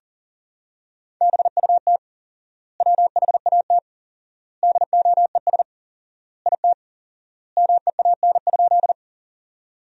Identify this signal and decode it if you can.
Morse code, heard as a keyed tone, about 40 wpm, 700 Hz. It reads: BUT WHAT DOES IT MEAN?
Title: 4x Word-Spacing - Sentences from Top 200 Words - 40wpm